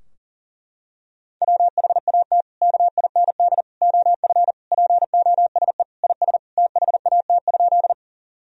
Morse code, heard as a keyed tone, about 40 wpm, 700 Hz. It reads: WHAT KIND OF POSE IS THAT?
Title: Sentences from Top 1000 Words - Review Part 2 - 40wpm